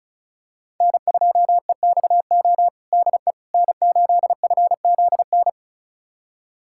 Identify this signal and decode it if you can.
Morse code, heard as a keyed tone, about 36 wpm, 700 Hz. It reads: N2EXO DE N8FZN
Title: USA Callsign DE Callsign - 5 Characters - 35wpm